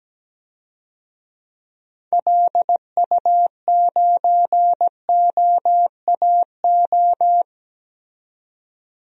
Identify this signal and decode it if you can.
Morse code, heard as a keyed tone, about 17 wpm, 700 Hz. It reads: LU9OAO
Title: DXCC-100 Call Signs - 6 Characters - 17wpm